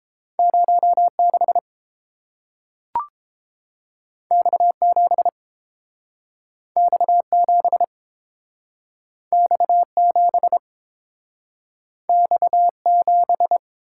Morse code, heard as a keyed tone, about 30 wpm, 700 Hz. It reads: 06 X7 X7 X7 X7